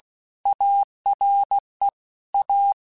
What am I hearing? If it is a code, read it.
Morse code, heard as a keyed tone, about 16 wpm, 800 Hz. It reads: ARE A
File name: Technical Subjects / Technical Subjects-mp3s / Sw Pwr Sup @15